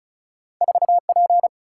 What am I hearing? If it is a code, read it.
Morse code, heard as a keyed tone, about 36 wpm, 700 Hz. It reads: 4 P
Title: Single-Letter Single-Number 35wpm